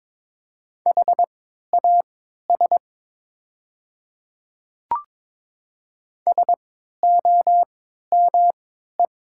Morse code, heard as a keyed tone, about 22 wpm, spaced as for 12 wpm, 700 Hz. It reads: HAS SOME
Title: 3x Character Spacing - Top 200 Words - 22wpm